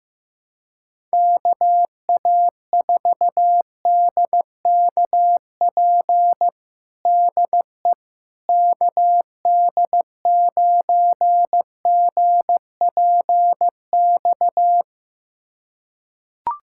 Morse code, heard as a keyed tone, about 15 wpm, 700 Hz. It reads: KA4DKP DE KD9GPX
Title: USA Callsign DE Callsign 15wpm